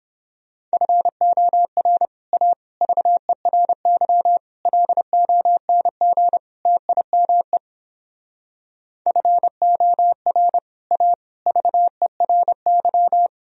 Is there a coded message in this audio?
Yes: Morse code from 0.7 to 13.3 s, heard as a keyed tone, about 31 wpm, 700 Hz. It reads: FOR A VERY LONG TIME FOR A VERY